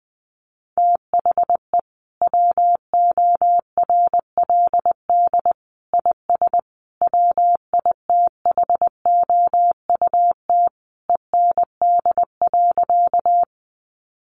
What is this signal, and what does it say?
Morse code, heard as a keyed tone, about 19 wpm, 700 Hz. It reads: THE WORLD IS WITHOUT END.
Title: Sentences from Top 600 Words 20wpm